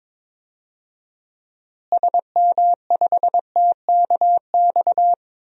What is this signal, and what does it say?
Morse code, heard as a keyed tone, about 23 wpm, 700 Hz. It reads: SM5TKX
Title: DXCC-100 Call Signs - 6 Characters - Repeated 2x - 22wpm